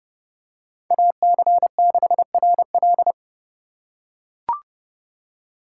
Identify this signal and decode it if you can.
Morse code, heard as a keyed tone, about 30 wpm, 700 Hz. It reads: AC6RL
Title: USA Call Signs - 5 Characters - 2x Character Spacing - 30wpm